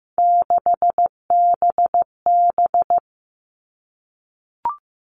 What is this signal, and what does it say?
Morse code, heard as a keyed tone, about 15 wpm, 700 Hz. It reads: 6BB
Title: All Cut Numbers - 3-Digit 15wpm